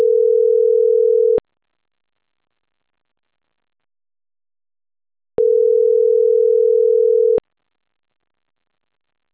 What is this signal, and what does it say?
Ringback tone (North America), 440 + 480 Hz.